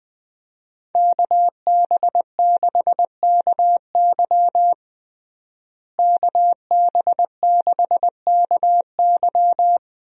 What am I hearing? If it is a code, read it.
Morse code, heard as a keyed tone, about 20 wpm, 700 Hz. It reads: KB6KY KB6KY
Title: USA Call Signs - 5 Characters - Repeated 3x - 20wpm